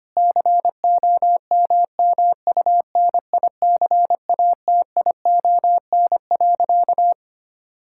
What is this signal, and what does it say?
Morse code, heard as a keyed tone, about 25 wpm, 700 Hz. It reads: COMMUNICATION.
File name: practice / 25wpm / Difficult Sentences with Hints - 25wpm